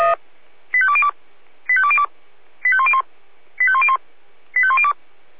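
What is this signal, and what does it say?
Five-tone selective call (ZVEI), heard as quick runs of five tones, tones of about 80 ms. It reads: 85202 85202 85101 85101 85102
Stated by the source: Five-tone ZVEI selective call received by SDR.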